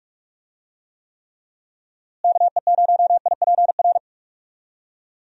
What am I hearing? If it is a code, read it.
Morse code, heard as a keyed tone, about 42 wpm, 700 Hz. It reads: KE0IPR
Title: USA Call Signs - 6 Characters - Repeated 3x - 45wpm